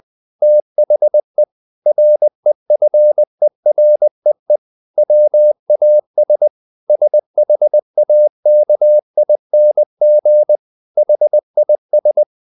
Morse code, heard as a keyed tone, about 20 wpm, 600 Hz. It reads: THE REFEREE WAS SHAKING HIS
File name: the old man and the sea part 2 20wpm